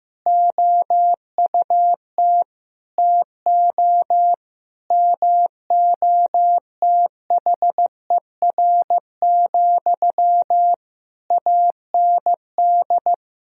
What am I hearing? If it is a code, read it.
Morse code, heard as a keyed tone, about 15 wpm, 700 Hz. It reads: OUT TO MOTHER, AND
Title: Sentences from Top 1000 Words - Review Part 1 - 15wpm